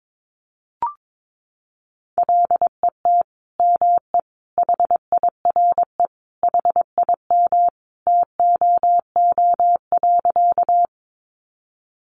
Morse code, heard as a keyed tone, about 22 wpm, 700 Hz. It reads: LET ME HIRE HIM TOO.
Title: Sentences from Top 1400 Words 22wpm